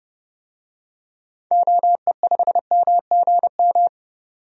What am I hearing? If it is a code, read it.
Morse code, heard as a keyed tone, about 30 wpm, 700 Hz. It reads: OE5MGM
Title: DXCC-100 Call Signs - 6 Characters - 30wpm